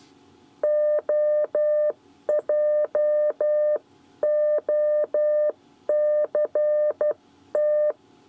Morse code, heard as a keyed tone, about 12 wpm, spaced as for 10 wpm, 600 Hz. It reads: OJOCT
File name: 8 bis lezione cw 1